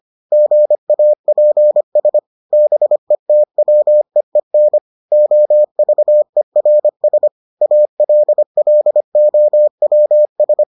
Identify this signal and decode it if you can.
Morse code, heard as a keyed tone, about 25 wpm, 600 Hz. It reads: GAPS BETWEEN OVERS ALLOWS